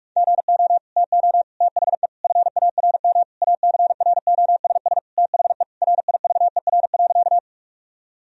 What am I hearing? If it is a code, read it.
Morse code, heard as a keyed tone, about 46 wpm, 700 Hz. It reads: GO TO THE FARM ACROSS THE RIVER.
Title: Sentences from Top 1000 Words - Review Part 2 - 45wpm